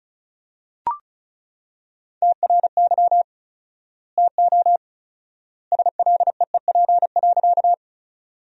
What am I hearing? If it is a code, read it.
Morse code, heard as a keyed tone, about 36 wpm, 700 Hz. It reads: TRY TO SLEEP.
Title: 4x Word-Spacing - Sentences from Top 500 Words - 35wpm